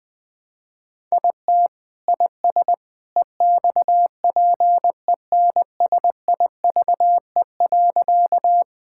Morse code, heard as a keyed tone, about 21 wpm, 700 Hz. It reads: IT IS EXPENSIVE.
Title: Sentences from Top 1000 Words - Review Part 6 - 20wpm